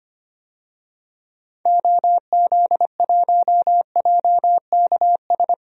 Morse code, heard as a keyed tone, about 25 wpm, 700 Hz. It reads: OZ1JKS
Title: DXCC-100 Call Signs - 6 Characters - Repeated 2x - 25wpm